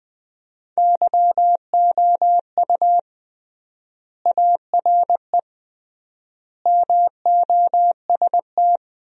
Morse code, heard as a keyed tone, about 20 wpm, 700 Hz. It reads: YOU ARE MOST
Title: 3x Word-Spacing - Sentences from Top 200 Words - 20wpm